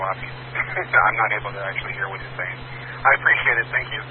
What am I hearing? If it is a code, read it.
Emergency Alert System (SAME) tones, heard as harsh buzzing bursts, with speech. It is heard from 0.5 to 3.4 s.